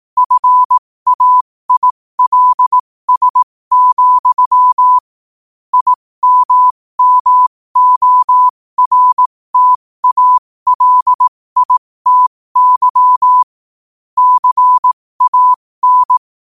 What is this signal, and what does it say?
Morse code, heard as a keyed tone, about 18 wpm, spaced as for 15 wpm, 1000 Hz. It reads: FAILS, IMMORTALITY CAN
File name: qotdfiles / 15wpm